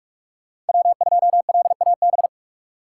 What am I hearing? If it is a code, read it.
Morse code, heard as a keyed tone, about 59 wpm, spaced as for 51 wpm, 700 Hz. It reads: W2LAB
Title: Speed Racing - USA 1x3 Call Signs 45wpm